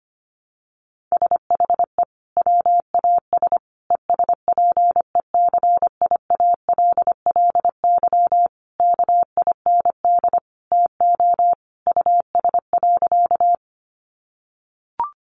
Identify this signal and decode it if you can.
Morse code, heard as a keyed tone, about 25 wpm, 700 Hz. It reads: SHE WAS ESPECIALLY KIND TO US.
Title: Sentences from Top 700 Words 25wpm